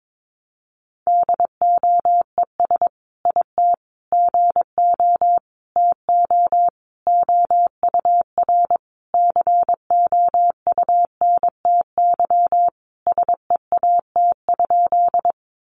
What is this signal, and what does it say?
Morse code, heard as a keyed tone, about 22 wpm, 700 Hz. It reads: DOES IT GO TO OUR COUNTY SEAT?
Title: Sentences from Top 1500 Words 22wpm